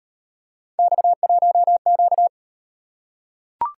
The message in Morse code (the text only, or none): X1Q